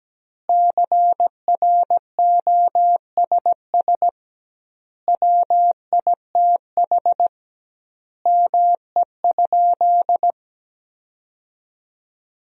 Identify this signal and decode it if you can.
Morse code, heard as a keyed tone, about 17 wpm, 700 Hz. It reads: CROSS WITH ME?
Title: 2x Word-Spacing - Sentences from Top 300 Words - 17wpm